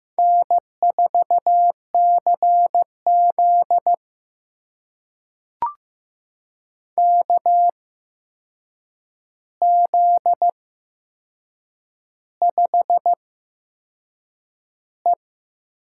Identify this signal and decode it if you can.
Morse code, heard as a keyed tone, about 15 wpm, spaced as for 6 wpm, 700 Hz. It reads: N4CZ KZ5E
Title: USA Call Signs - 8x Character Spacing - 15wpm